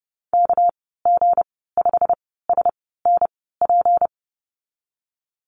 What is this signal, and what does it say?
Morse code, heard as a keyed tone, about 30 wpm, spaced as for 17 wpm, 700 Hz. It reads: KG5SNP
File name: Lesson 37g - Letter x - Call Signs 30wpm